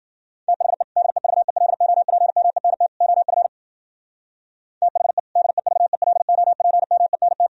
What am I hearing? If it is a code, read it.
Morse code, heard as a keyed tone, about 50 wpm, 700 Hz. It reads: THE DEVELOPMENT OF THE DEVELOPMENT